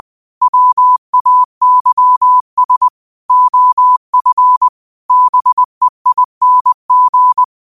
Morse code, heard as a keyed tone, about 20 wpm, 1000 Hz. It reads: WAYS OF BEING
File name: qotdfiles / 20wpm